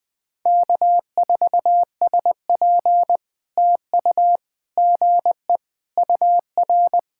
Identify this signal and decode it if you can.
Morse code, heard as a keyed tone, about 20 wpm, 700 Hz. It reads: K4SP TU GE UR
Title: POTA QSO Practice - 20wpm